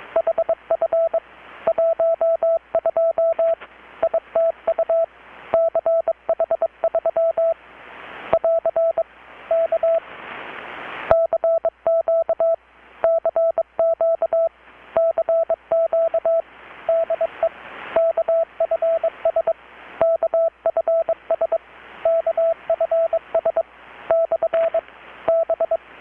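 Morse code, heard as a keyed tone, about 22 wpm, 660 Hz, with radio noise. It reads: HF 12 ITU CH3 <AR> K CQ CQ CQ DE KFS KFS KFS / B
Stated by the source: Morse from KPH's Night of Nights XX maritime radiotelegraphy event, received in Chiapas, Mexico about 3,800 km away on a Kenwood R-600 with a grounded 10 m long-wire antenna.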